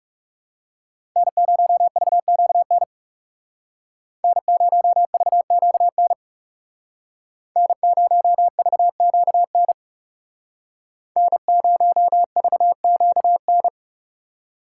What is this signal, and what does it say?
Morse code, heard as a keyed tone, about 37 wpm, 700 Hz. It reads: N0VQN N0VQN N0VQN N0VQN
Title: Speed Racing - USA 1x3 Call Signs 30wpm